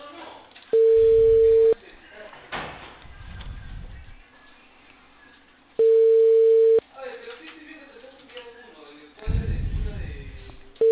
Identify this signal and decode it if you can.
Ringback tone (China, 450 Hz), with speech.